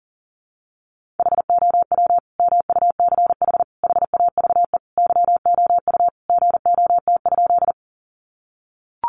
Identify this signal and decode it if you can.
Morse code, heard as a keyed tone, about 40 wpm, 700 Hz. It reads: HOW MUCH HAVE YOU GOT?
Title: Sentences from Top 300 Words 40wpm